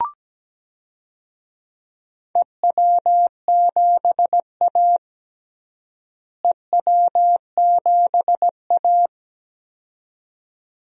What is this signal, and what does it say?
Morse code, heard as a keyed tone, about 17 wpm, 700 Hz. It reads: EW7A EW7A
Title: DXCC-100 Call Signs - 4 Characters - Repeated 2x - 17wpm